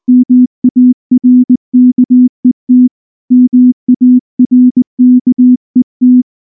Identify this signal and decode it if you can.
Morse code, heard as a keyed tone, about 19 wpm, 250 Hz. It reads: MARKET MARKET
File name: morse / morse-training-20WPM-3